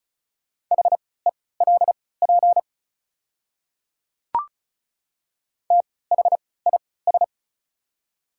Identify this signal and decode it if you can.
Morse code, heard as a keyed tone, about 36 wpm, spaced as for 20 wpm, 690 Hz. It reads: HELP THIS
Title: 3x Character Spacing - Top 300 Words - 35wpm